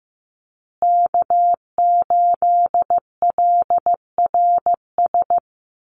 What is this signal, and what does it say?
Morse code, heard as a keyed tone, about 15 wpm, 700 Hz. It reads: K8LRS
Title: USA Call Signs Repeated 2x 15wpm